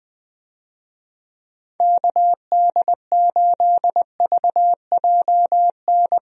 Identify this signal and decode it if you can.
Morse code, heard as a keyed tone, about 20 wpm, 700 Hz. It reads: KD8VJN
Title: USA Call Signs - 6 Characters - 20wpm